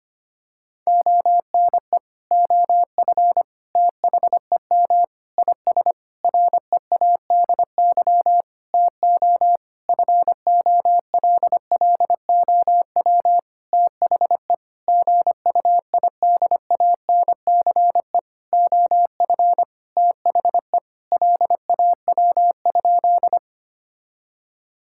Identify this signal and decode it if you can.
Morse code, heard as a keyed tone, about 25 wpm, 700 Hz. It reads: ONE OF THEM IS READY TO FOLLOW THE GUIDANCE OF THE LAW?
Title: Sentences from Top 1600 Words 25wpm